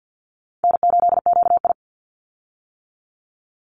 Morse code, heard as a keyed tone, about 50 wpm, 700 Hz. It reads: N8QI
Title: USA Call Signs 50wpm